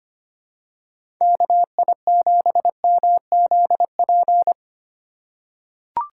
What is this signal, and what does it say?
Morse code, heard as a keyed tone, about 25 wpm, 700 Hz. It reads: KI7MZP
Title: USA Call Signs - 6 Characters - 2x Character Spacing - 25wpm